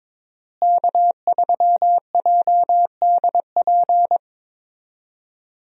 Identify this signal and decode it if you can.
Morse code, heard as a keyed tone, about 22 wpm, 700 Hz. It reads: K3JDP
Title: USA Call Signs - 5 Characters - Repeated 2x - 22wpm